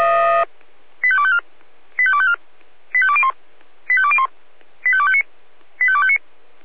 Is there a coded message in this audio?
Yes: five-tone selective call (ZVEI) from 1.0 to 6.2 s, heard as quick runs of five tones, tones of about 70 ms. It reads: 85254 85254 85201 85201 85259 85259